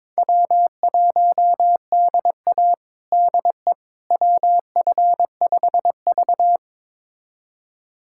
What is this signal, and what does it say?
Morse code, heard as a keyed tone, about 22 wpm, 700 Hz. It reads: W1DA DE WF5V
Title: USA Callsign DE Callsign - 4 Characters - 22wpm